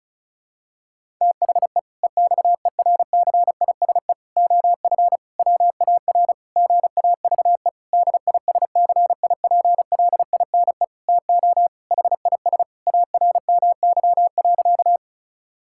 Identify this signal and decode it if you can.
Morse code, heard as a keyed tone, about 34 wpm, 700 Hz. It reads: THE EXERCISE OF WAR GAVE DISCIPLINE TO HIS ARMY.
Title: Sentences from Top 1000 Words - Review Part 6 - 35wpm